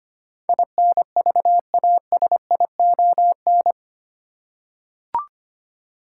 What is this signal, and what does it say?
Morse code, heard as a keyed tone, about 25 wpm, 700 Hz. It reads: INVASION
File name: Eight Letter Words - 25wpm